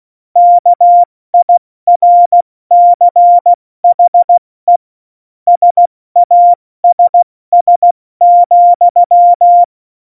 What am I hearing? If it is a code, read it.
Morse code, heard as a keyed tone, about 16 wpm, 700 Hz. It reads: KIRCHE SASS,